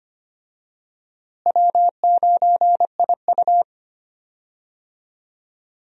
Morse code, heard as a keyed tone, about 25 wpm, 700 Hz. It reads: W9IU
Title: USA Call Signs - 4 Characters - Repeated 2x - 25wpm